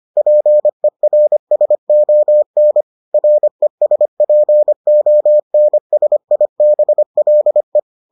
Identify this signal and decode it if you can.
Morse code, heard as a keyed tone, about 25 wpm, 600 Hz. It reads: PERSON RESPONSIBLE